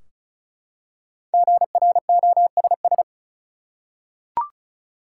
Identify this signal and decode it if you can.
Morse code, heard as a keyed tone, about 34 wpm, 700 Hz. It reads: GROSS
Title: Five Letter Words - 35wpm